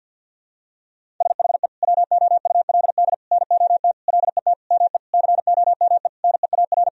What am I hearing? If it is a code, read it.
Morse code, heard as a keyed tone, about 53 wpm, 700 Hz. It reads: SHE WOULD NOT LET ME COME NEAR
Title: Sentences from Top 1000 Words - Review Part 1 - 50wpm